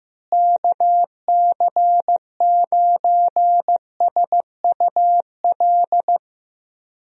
Morse code, heard as a keyed tone, about 15 wpm, 700 Hz. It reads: KC9SUL
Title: USA Call Signs - 6 Characters - Repeated 2x - 15wpm